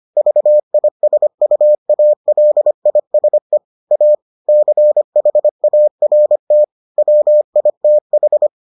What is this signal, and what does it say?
Morse code, heard as a keyed tone, about 25 wpm, 600 Hz. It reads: VISUALISE A CHART WITH